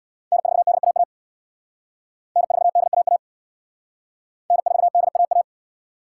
Morse code, heard as a keyed tone, about 60 wpm, 700 Hz. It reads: N4DAA N4DAA N4DAA
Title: Speed Racing - USA Call Signs 50wpm